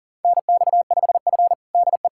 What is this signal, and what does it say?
Morse code, heard as a keyed tone, about 40 wpm, 700 Hz. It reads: NX5F DE
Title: USA Callsign DE Callsign - 4 Characters - 40wpm